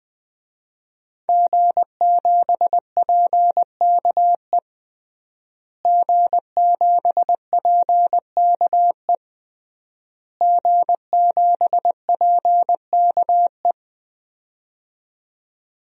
Morse code, heard as a keyed tone, about 20 wpm, 700 Hz. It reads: G7PKE G7PKE G7PKE
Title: DXCC-100 Call Signs - 5 Characters - Repeated 3x - 20wpm